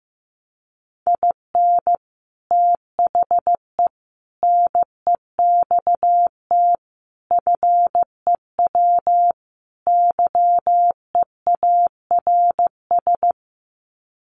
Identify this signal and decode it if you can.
Morse code, heard as a keyed tone, about 15 wpm, 700 Hz. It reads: IN THE NEXT FEW YEARS
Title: Sets of 5 Words Encore 15wpm